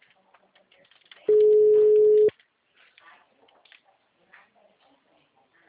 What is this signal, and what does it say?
Ringback tone (Europe, 425 Hz), with speech.